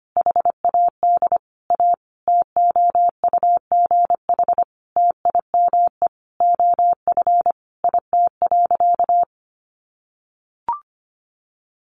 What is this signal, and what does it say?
Morse code, heard as a keyed tone, about 25 wpm, 700 Hz. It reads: HAD A TOUGH TIME OF IT.
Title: Sentences from Top 1200 Words 25wpm